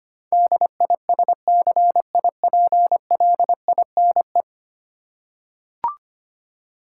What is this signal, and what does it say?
Morse code, heard as a keyed tone, about 25 wpm, 700 Hz. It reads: DISCIPLINE
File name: Ten Letter Words - 25wpm